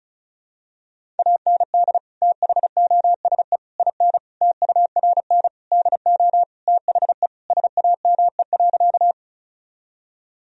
Morse code, heard as a keyed tone, about 36 wpm, 700 Hz. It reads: AND THOSE IN TURN DO THE SAME.
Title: Sentences from Top 1000 Words - Review Part 1 - 35wpm